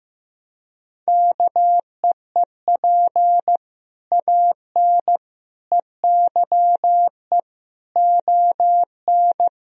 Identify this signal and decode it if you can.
Morse code, heard as a keyed tone, about 15 wpm, 700 Hz. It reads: KEEP AN EYE ON
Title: Sentences from Top 1000 Words - Review Part 1 - 15wpm